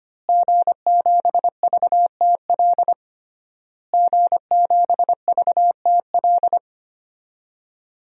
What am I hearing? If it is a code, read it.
Morse code, heard as a keyed tone, about 25 wpm, 700 Hz. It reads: G7VTL G7VTL
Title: DXCC-100 Call Signs - 5 Characters - Repeated 3x - 25wpm